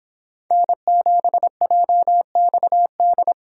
Morse code, heard as a keyed tone, about 25 wpm, 700 Hz. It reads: N7JXD